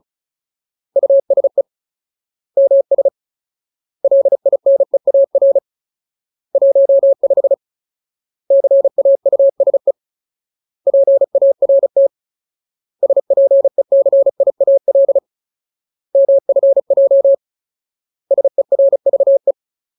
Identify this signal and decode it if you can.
Morse code, heard as a keyed tone, about 34 wpm, 550 Hz. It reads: USE MS LINEAR 15 CAUSE PART SPECIAL MFJ SERVE
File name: Ultimate Word List / 35WPM Ultimate